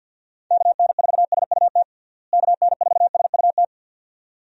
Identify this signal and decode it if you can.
Morse code, heard as a keyed tone, about 50 wpm, 700 Hz. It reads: KN4SUT KN4SUT
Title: USA Call Signs - 6 Characters - Repeated 2x - 50wpm